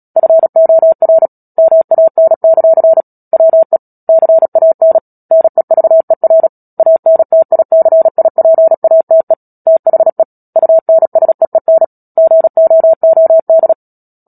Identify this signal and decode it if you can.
Morse code, heard as a keyed tone, about 37 wpm, 650 Hz. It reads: FOR MAN; WE CAN NEVER ANTICIPATE THE UNSEEN GOOD
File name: Book Two / 10-Chapter 10 - Epilogue - 36 WPM